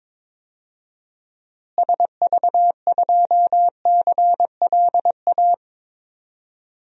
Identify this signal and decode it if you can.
Morse code, heard as a keyed tone, about 22 wpm, 700 Hz. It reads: SV2CLA
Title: DXCC-100 Call Signs - 6 Characters - Repeated 2x - 22wpm